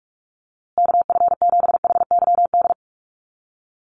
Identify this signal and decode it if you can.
Morse code, heard as a keyed tone, about 44 wpm, 700 Hz. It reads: KF7HYD
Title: USA Call Signs Repeated 2x 45wpm